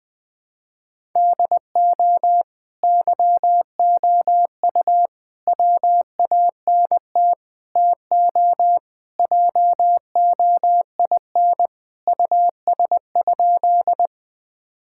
Morse code, heard as a keyed tone, about 20 wpm, 700 Hz. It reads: DO YOU WANT TO JOIN US?
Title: Sentences with 4-Word Phrases - 20wpm